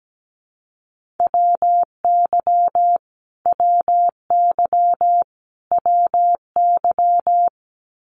Morse code, heard as a keyed tone, about 17 wpm, 700 Hz. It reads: WY WY WY
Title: US States Repeated 3x 17wpm